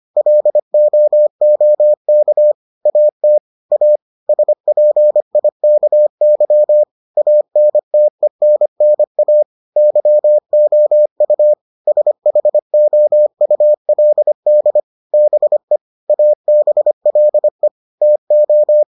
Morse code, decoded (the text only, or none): LOOK AT A SPIKY ANTENNA YOU SHOULD BE ABLE TO